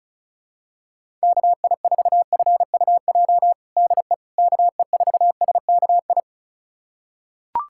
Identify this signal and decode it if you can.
Morse code, heard as a keyed tone, about 35 wpm, 700 Hz. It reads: KI4FUJ DE KE4SKI
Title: USA Callsign DE Callsign - 6 Characters - 35wpm